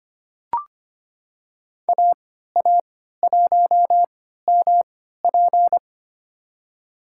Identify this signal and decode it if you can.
Morse code, heard as a keyed tone, about 25 wpm, spaced as for 14 wpm, 700 Hz. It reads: AA1MP
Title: USA Call Signs - 5 Characters - 3x Character Spacing - 25wpm